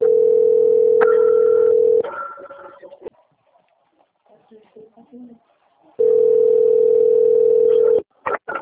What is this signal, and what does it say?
Ringback tone (North America), 440 + 480 Hz, with speech.
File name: out- - - - - .29394